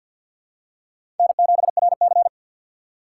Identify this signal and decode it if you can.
Morse code, heard as a keyed tone, about 48 wpm, 700 Hz. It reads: N7RC
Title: USA Call Signs - 4 Characters - 3x Character Spacing - 50wpm